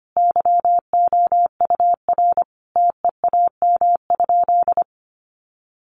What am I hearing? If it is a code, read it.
Morse code, heard as a keyed tone, about 25 wpm, 700 Hz. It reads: YOUR TEAM?
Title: Sentences from Top 1500 Words 25wpm